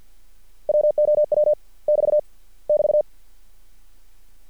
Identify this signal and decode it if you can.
Morse code, heard as a keyed tone, about 50 wpm, 600 Hz. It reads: WOW <BT> <BT>
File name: archived / CRSnet